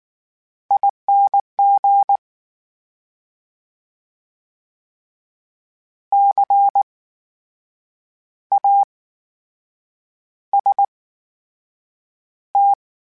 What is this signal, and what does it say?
Morse code, heard as a keyed tone, about 19 wpm, spaced as for 7 wpm, 800 Hz. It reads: ING CAST
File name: Special INGs / Special INGs-mp3s / VerySlowToFaster Ing @18